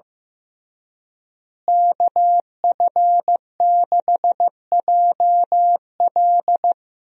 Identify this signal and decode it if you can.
Morse code, heard as a keyed tone, about 15 wpm, 700 Hz. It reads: KF6JL
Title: USA Call Signs - 5 Characters - Repeated 3x - 15wpm